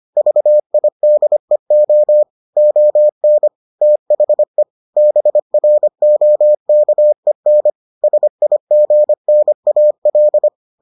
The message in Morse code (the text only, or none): VIDEO ON THE BROKEN SIGNAL